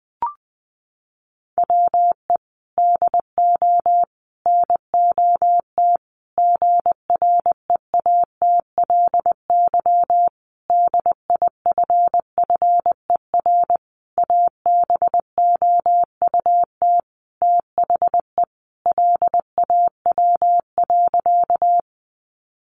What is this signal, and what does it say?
Morse code, heard as a keyed tone, about 20 wpm, 700 Hz. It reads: WE DO NOT GREATLY DIFFER ABOUT THE LAW.
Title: Sentences from Top 1400 Words 20wpm